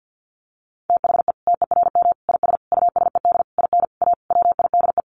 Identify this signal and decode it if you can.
Morse code, heard as a keyed tone, about 49 wpm, 700 Hz. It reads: THE TERM IS USED IN A WIDE
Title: Sentences from Top 900 Words 50wpm